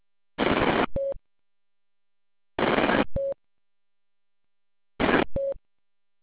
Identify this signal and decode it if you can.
There is no code.